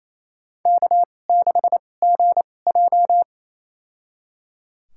Morse code, heard as a keyed tone, about 28 wpm, spaced as for 20 wpm, 700 Hz. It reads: K6GJ
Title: USA Call Signs - 2x Character Spacing - 28wpm